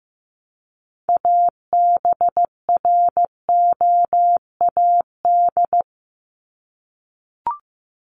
Morse code, heard as a keyed tone, about 15 wpm, 700 Hz. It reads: ABROAD
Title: Top 1751-2000 Words 15wpm